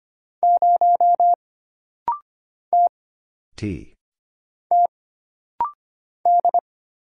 Morse code, heard as a keyed tone, about 25 wpm, 700 Hz. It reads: 0 T T D
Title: Single Letter-Number Alt - Rapid-Fire 25wpm